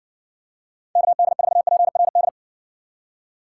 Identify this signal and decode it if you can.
Morse code, heard as a keyed tone, about 60 wpm, 700 Hz. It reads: KD4PRD